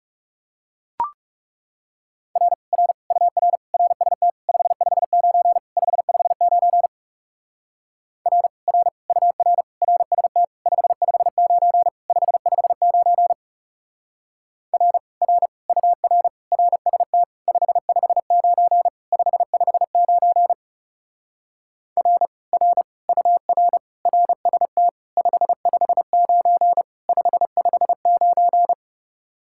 Morse code, heard as a keyed tone, about 38 wpm, 700 Hz. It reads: R R UR RST 559 559 R R UR RST 559 559 R R UR RST 559 559 R R UR RST 559 559